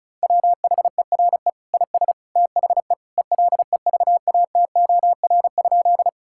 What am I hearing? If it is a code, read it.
Morse code, heard as a keyed tone, about 34 wpm, 700 Hz. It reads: WHERE IS THE ELEVATOR?
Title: Sentences from Top 1600 Words 35wpm